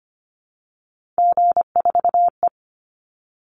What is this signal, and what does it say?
Morse code, heard as a keyed tone, about 25 wpm, 700 Hz. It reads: G4E
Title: Call Signs - 25wpm